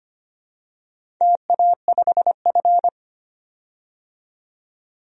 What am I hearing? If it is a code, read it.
Morse code, heard as a keyed tone, about 25 wpm, 700 Hz. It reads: TA5F
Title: DXCC-100 Call Signs - 4 Characters - 25wpm